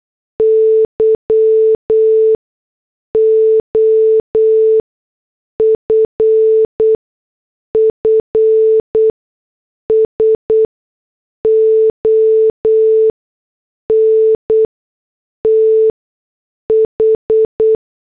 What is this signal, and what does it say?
Morse code, heard as a keyed tone, about 8 wpm, spaced as for 6 wpm, 440 Hz. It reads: Y OFFS ON TH